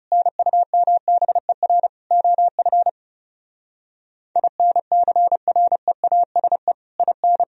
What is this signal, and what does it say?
Morse code, heard as a keyed tone, about 40 wpm, 700 Hz. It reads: NUMBER OF INCREASE IN